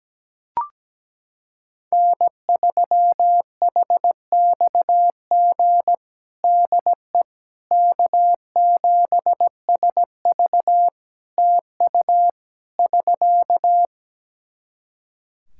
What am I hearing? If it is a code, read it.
Morse code, heard as a keyed tone, about 17 wpm, 700 Hz. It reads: N3HXG DE K7SV TU <SK>
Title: K1USN SST QSO Practice 17wpm